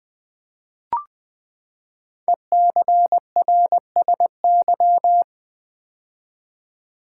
Morse code, heard as a keyed tone, about 20 wpm, 700 Hz. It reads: ECRSY
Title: Code Groups - 5 Letters - 20wpm